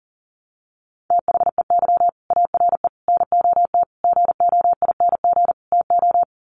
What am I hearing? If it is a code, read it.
Morse code, heard as a keyed tone, about 40 wpm, 700 Hz. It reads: THEY ARE NOT GOING TO
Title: Sets of 5 Words Encore2 40wpm